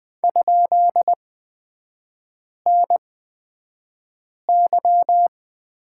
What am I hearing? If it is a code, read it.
Morse code, heard as a keyed tone, about 20 wpm, spaced as for 5 wpm, 700 Hz. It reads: ?NY
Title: 224 Lesson 36e - BT - Three Characters 20wpm